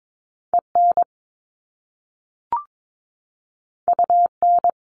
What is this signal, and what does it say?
Morse code, heard as a keyed tone, about 22 wpm, 700 Hz. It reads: EN UN
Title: Top Two Letter Combinations in Words 22wpm